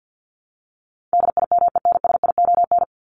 Morse code, heard as a keyed tone, about 50 wpm, 700 Hz. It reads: DIMENSION
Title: Top 1501-1750 Words 50wpm